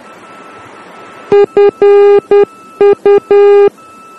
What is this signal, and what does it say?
Morse code, heard as a keyed tone, about 10 wpm, 400 Hz. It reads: FU
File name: germany / mx / FU-D